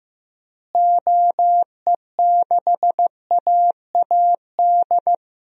Morse code, heard as a keyed tone, about 15 wpm, 700 Hz. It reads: OE6AAD
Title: DXCC-100 Call Signs - 6 Characters - Repeated 3x - 15wpm